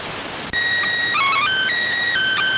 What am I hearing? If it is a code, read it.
SSTV (slow-scan television), PD 290.